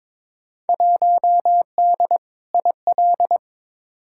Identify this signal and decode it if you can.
Morse code, heard as a keyed tone, about 22 wpm, 700 Hz. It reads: 1D IL
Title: Field Day Exchanges - 22wpm